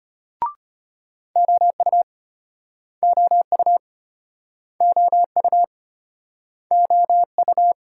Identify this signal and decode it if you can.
Morse code, heard as a keyed tone, about 35 wpm, 700 Hz. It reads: OU OU OU OU